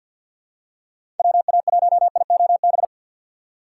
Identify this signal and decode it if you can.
Morse code, heard as a keyed tone, about 48 wpm, 700 Hz. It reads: WA1IOB
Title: USA Call Signs - 6 Characters - 4x Character Spacing - 50wpm